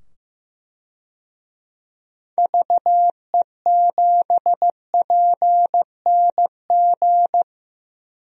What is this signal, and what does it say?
Morse code, heard as a keyed tone, about 15 wpm, 700 Hz. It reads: VE7PNG
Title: DXCC-100 Call Signs - 6 Characters - 15wpm